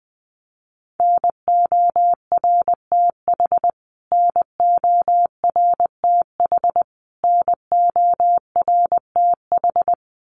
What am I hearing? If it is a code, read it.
Morse code, heard as a keyed tone, about 20 wpm, 700 Hz. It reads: NORTH NORTH NORTH
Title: Top 500 Words Repeated 20wpm